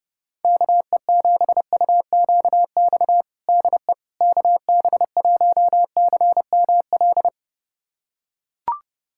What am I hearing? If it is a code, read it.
Morse code, heard as a keyed tone, about 30 wpm, 700 Hz. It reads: KE7UQX DE KB1CML
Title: USA Callsign DE Callsign 30wpm